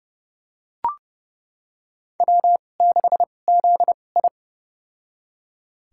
Morse code, heard as a keyed tone, about 30 wpm, spaced as for 22 wpm, 700 Hz. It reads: W6ZI
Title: USA Call Signs - 4 Characters - 2x Character Spacing - 30wpm